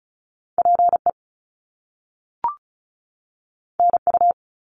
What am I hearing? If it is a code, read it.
Morse code, heard as a keyed tone, about 36 wpm, 700 Hz. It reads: PE NU
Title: Canadian Provinces and Territories 35wpm